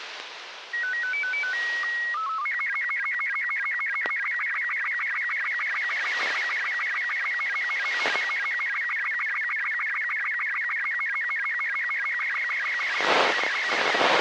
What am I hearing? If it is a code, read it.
SSTV (slow-scan television), Robot 36.
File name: arissat-aug05-2011